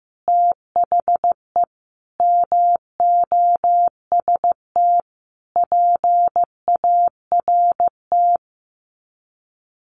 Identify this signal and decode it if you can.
Morse code, heard as a keyed tone, about 14 wpm, 700 Hz. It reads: THE MOST PART
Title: Sets of 5 Words Encore 15wpm